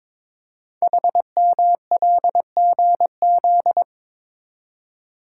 Morse code, heard as a keyed tone, about 22 wpm, 700 Hz. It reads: HMLGZ
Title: Code Groups - 5 Letters - 22wpm